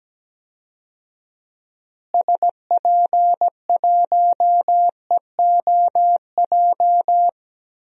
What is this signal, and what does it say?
Morse code, heard as a keyed tone, about 17 wpm, 700 Hz. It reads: SP1EOJ
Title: DXCC-100 Call Signs - 6 Characters - Repeated 3x - 17wpm